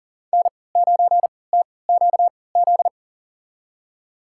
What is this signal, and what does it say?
Morse code, heard as a keyed tone, about 40 wpm, spaced as for 23 wpm, 700 Hz. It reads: N9TQZ
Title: USA Call Signs - 5 Characters - 3x Character Spacing - 40wpm